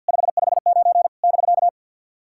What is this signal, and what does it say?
Morse code, heard as a keyed tone, about 50 wpm, 700 Hz. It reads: 559 <BK>
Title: SOTA QSO Practice 50wpm